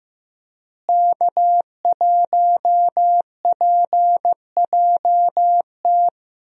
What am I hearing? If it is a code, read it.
Morse code, heard as a keyed tone, about 16 wpm, 700 Hz. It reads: K1PJT
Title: USA Call Signs - 5 Characters - Repeated 3x - 15wpm